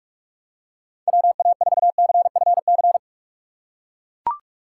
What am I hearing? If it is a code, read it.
Morse code, heard as a keyed tone, about 42 wpm, 700 Hz. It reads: WA4CFC
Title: USA Call Signs - 6 Characters - 6x Character Spacing - 45wpm